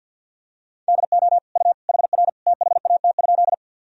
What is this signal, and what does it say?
Morse code, heard as a keyed tone, about 52 wpm, 700 Hz. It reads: DO U HR THAT?
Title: Sentences from Top 300 Words and Abbreviations 50wpm